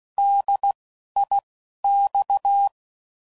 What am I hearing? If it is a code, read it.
Morse code, heard as a keyed tone, about 16 wpm, 800 Hz. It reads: D I X
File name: Prefixes / By First Letter / Ds